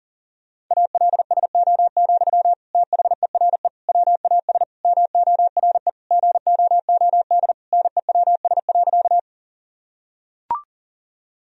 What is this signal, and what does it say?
Morse code, heard as a keyed tone, about 40 wpm, 700 Hz. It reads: ALSO, THERE WAS MORE GOOD NEWS.
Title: Sentences from Top 1000 Words - Review Part 1 - 40wpm